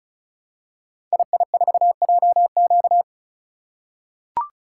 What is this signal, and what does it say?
Morse code, heard as a keyed tone, about 36 wpm, 700 Hz. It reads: II4JQ
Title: Code Groups - 5 Letter:Numbers - 35wpm